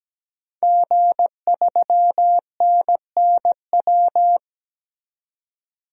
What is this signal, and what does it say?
Morse code, heard as a keyed tone, about 17 wpm, 700 Hz. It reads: G3NNW
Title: DXCC-100 Call Signs - 5 Characters - 17wpm